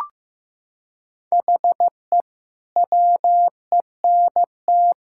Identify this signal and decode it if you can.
Morse code, heard as a keyed tone, about 16 wpm, 700 Hz. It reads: HE WENT
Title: Sentences from Top 1000 Words - Review Part 5 - 15wpm